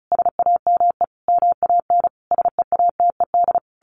Morse code, heard as a keyed tone, about 34 wpm, 700 Hz. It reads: SAME MAN SEATED
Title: A Sound Of Thunder 35wpm